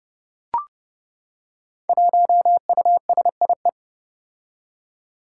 Morse code, heard as a keyed tone, about 30 wpm, 700 Hz. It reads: 1USIE
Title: Five Challenging Letter-Number Combinations 30wpm